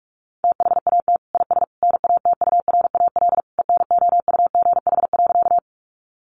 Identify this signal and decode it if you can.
Morse code, heard as a keyed tone, about 44 wpm, 700 Hz. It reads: THAT IS NATURAL ENOUGH.
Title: Sentences from Top 700 Words 45wpm